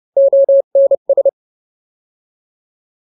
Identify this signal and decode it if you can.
Morse code, heard as a keyed tone, about 30 wpm, 550 Hz. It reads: ONS